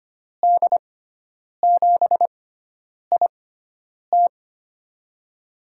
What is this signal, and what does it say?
Morse code, heard as a keyed tone, about 25 wpm, spaced as for 9 wpm, 700 Hz. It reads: D7IT
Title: USA Call Signs - 6x Character Spacing - 25wpm